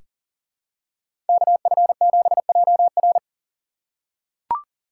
Morse code, heard as a keyed tone, about 40 wpm, 700 Hz. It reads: KF7JR
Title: USA Call Signs - 5 Characters - 3x Character Spacing - 40wpm